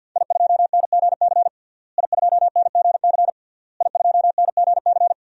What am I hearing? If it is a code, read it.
Morse code, heard as a keyed tone, about 52 wpm, 700 Hz. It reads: I2NGC I2NGC I2NGC
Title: DXCC-100 Call Signs - 5 Characters - Repeated 3x - 50wpm